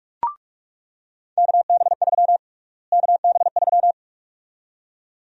Morse code, heard as a keyed tone, about 41 wpm, 700 Hz. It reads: KB3 KB3
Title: USA Call Sign Prefixes - Repeated 2x - 45wpm